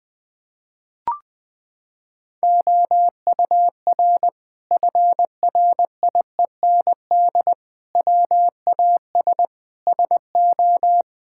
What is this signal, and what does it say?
Morse code, heard as a keyed tone, about 19 wpm, 700 Hz. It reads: OUR FRIEND WAS SO
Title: Sentences from Top 1000 Words - Review Part 5 - 20wpm